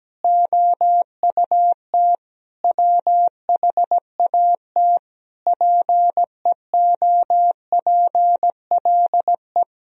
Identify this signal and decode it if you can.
Morse code, heard as a keyed tone, about 17 wpm, 700 Hz. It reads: OUT WHAT PEOPLE
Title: Sentences from Top 1000 Words - Review Part 4 - 17wpm